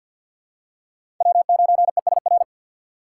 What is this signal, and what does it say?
Morse code, heard as a keyed tone, about 47 wpm, 700 Hz. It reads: W9ESR
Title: USA Call Signs - 5 Characters - 6x Character Spacing - 50wpm